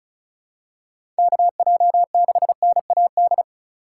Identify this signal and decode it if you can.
Morse code, heard as a keyed tone, about 34 wpm, 700 Hz. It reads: KJ6NAD
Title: USA Call Signs - 4x Character Spacing - 35wpm